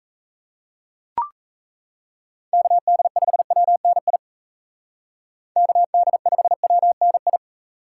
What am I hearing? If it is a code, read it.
Morse code, heard as a keyed tone, about 35 wpm, 700 Hz. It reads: KD5WNI KD5WNI
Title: Speed Racing - USA 2x3 Call Signs 28wpm